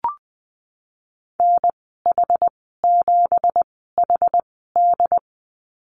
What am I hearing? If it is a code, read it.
Morse code, heard as a keyed tone, about 19 wpm, spaced as for 14 wpm, 700 Hz. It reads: NH7HD
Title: 231 Lesson 37g - Letter x - Call Signs 20wpm